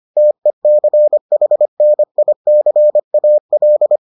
Morse code, heard as a keyed tone, about 25 wpm, 600 Hz. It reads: TECHNICAL